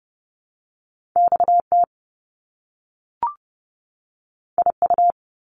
Morse code, heard as a keyed tone, about 30 wpm, 700 Hz. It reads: XT IU
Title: Top Two Letter Combinations in Words 30wpm